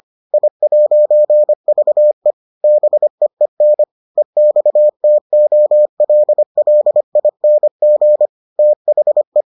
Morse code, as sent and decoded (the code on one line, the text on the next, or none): .. .----. ...- . / -... . -. / . -..- - --- .-.. .-.. .. -. --. / - .... .
I'VE BEEN EXTOLLING THE